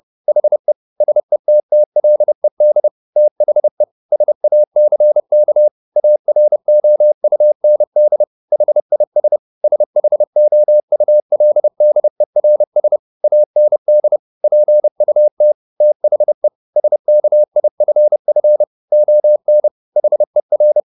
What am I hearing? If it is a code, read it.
Morse code, heard as a keyed tone, about 30 wpm, 600 Hz. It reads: HE SETTLED THE SACK AROUND HIS SHOULDERS AND PUT THE SKIFF ON HER